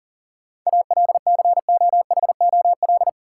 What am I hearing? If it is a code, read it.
Morse code, heard as a keyed tone, about 40 wpm, 700 Hz. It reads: ALCOHOL
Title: Sentences from Top 1400 Words - New Word Drill - 40wpm